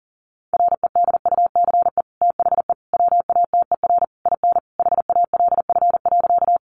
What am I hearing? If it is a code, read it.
Morse code, heard as a keyed tone, about 40 wpm, 700 Hz. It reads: REDUCE THE WATER IN HALF.
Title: Sentences from Top 1100 Words 40wpm